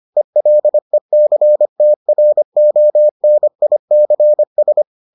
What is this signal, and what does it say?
Morse code, heard as a keyed tone, about 25 wpm, 600 Hz. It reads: ELECTRONICS